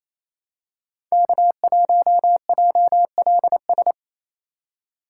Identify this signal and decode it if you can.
Morse code, heard as a keyed tone, about 28 wpm, 700 Hz. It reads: K1JLS
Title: USA Call Signs - 5 Characters - 4x Character Spacing - 28wpm